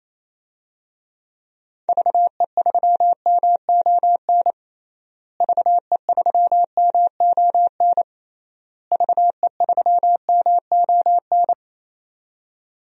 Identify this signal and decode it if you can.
Morse code, heard as a keyed tone, about 28 wpm, 700 Hz. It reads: VE3MON VE3MON VE3MON
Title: DXCC-100 Call Signs - 6 Characters - Repeated 3x - 28wpm